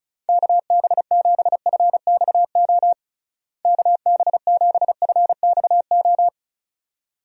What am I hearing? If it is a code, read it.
Morse code, heard as a keyed tone, about 34 wpm, 700 Hz. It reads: KB7FXO KB7FXO
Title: USA Call Signs - 6 Characters - Repeated 2x - 35wpm